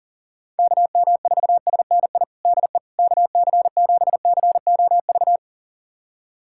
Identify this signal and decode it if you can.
Morse code, heard as a keyed tone, about 40 wpm, 700 Hz. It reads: KM4SNI DE KC7COV
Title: USA Callsign DE Callsign 40wpm